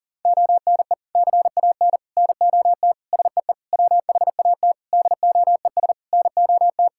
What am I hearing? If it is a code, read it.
Morse code, heard as a keyed tone, about 40 wpm, 700 Hz. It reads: ONE CAN NOT SEE WHAT DOES NOT
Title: Sentences from Top 1000 Words - Review Part 5 - 40wpm